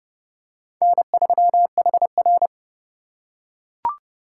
Morse code, heard as a keyed tone, about 30 wpm, 700 Hz. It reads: N3HR